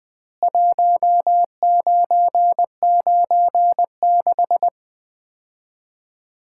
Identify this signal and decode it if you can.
Morse code, heard as a keyed tone, about 21 wpm, 700 Hz. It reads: 1996
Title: Common QSO Phrases - 3x Word-Spacing - 20wpm